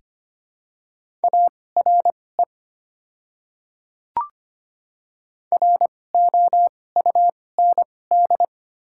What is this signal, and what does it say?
Morse code, heard as a keyed tone, about 25 wpm, spaced as for 18 wpm, 700 Hz. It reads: ARE ROUND
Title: 2x Character Spacing - Top 500 Words - 25wpm